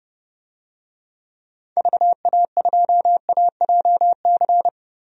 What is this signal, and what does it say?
Morse code, heard as a keyed tone, about 30 wpm, 700 Hz. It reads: VA2AJC
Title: DXCC-100 Call Signs - 6 Characters - 30wpm